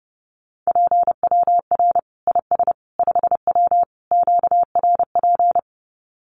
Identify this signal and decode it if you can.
Morse code, heard as a keyed tone, about 30 wpm, 700 Hz. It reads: PWR IS 5W QRP
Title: Common QSO Phrases 30wpm